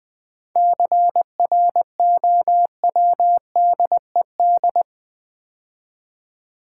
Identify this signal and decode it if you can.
Morse code, heard as a keyed tone, about 20 wpm, 690 Hz. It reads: CROWDED
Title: 7 Letter Words - First 3 Letters Given - 20wpm